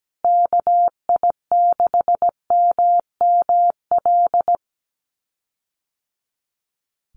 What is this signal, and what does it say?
Morse code, heard as a keyed tone, about 17 wpm, 700 Hz. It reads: KI6MML
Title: USA Call Signs Repeated 3x 17wpm